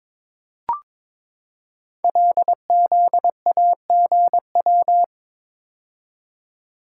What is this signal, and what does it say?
Morse code, heard as a keyed tone, about 23 wpm, 700 Hz. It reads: LZAGW
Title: Code Groups - 5 Letter:Numbers - 22wpm